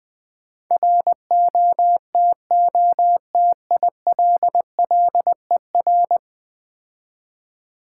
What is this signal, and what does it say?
Morse code, heard as a keyed tone, about 22 wpm, 700 Hz. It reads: ROTOTILLER
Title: Difficult Words - Speed-Racing with Hints - 20wpm